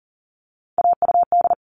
Morse code, heard as a keyed tone, about 40 wpm, 700 Hz. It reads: AUD
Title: All Cut Numbers - 3-Digit 40wpm